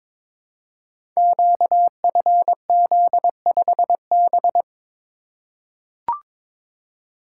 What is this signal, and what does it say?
Morse code, heard as a keyed tone, about 22 wpm, 700 Hz. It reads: QFZ5B
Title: Code Groups - 5 Letter:Numbers - 22wpm